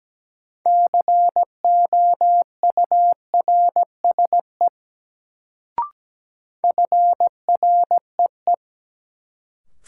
Morse code, heard as a keyed tone, about 17 wpm, 700 Hz. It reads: COURSE FREE
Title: Sentences from Top 500 Words - New Word Drill - 17wpm